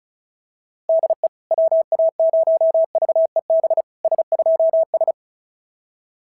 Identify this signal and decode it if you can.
Morse code, heard as a keyed tone, about 36 wpm, 690 Hz. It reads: DE WA0VEB S2S
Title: SOTA QSO Practice 35wpm